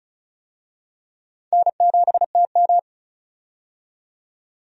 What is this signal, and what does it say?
Morse code, heard as a keyed tone, about 36 wpm, 700 Hz. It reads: N7TM
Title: USA Call Signs - 4 Characters - 35wpm